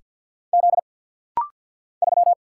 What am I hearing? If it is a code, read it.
Morse code, heard as a keyed tone, about 50 wpm, 690 Hz. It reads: Z 3
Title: Single Letter-Number Alt - Rapid-Fire 50wpm